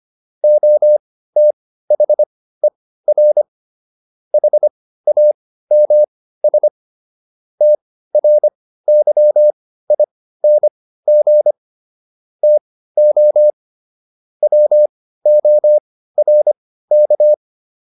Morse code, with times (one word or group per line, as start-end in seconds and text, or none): 0.4-3.4 OTHER
4.3-6.7 HAMS
7.6-11.5 TRYING
12.4-13.5 TO
14.4-17.3 WORK